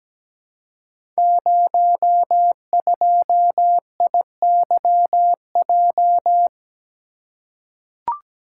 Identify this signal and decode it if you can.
Morse code, heard as a keyed tone, about 17 wpm, 700 Hz. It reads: 02IYJ
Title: Code Groups - 5 Letter:Numbers - 17wpm